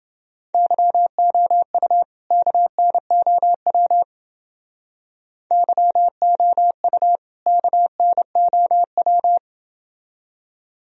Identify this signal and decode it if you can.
Morse code, heard as a keyed tone, about 28 wpm, 700 Hz. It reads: YOU KNOW YOU KNOW